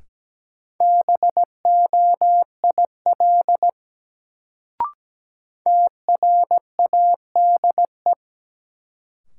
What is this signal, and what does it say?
Morse code, heard as a keyed tone, about 17 wpm, 700 Hz. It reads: BOIL TRADE
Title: Sets of 5 Words Encore - New Word Drill - 17wpm